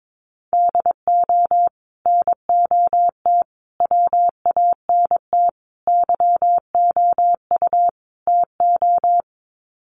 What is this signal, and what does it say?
Morse code, heard as a keyed tone, about 22 wpm, 700 Hz. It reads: DO NOT WANT YOU TO
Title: Sets of 5 Words 22wpm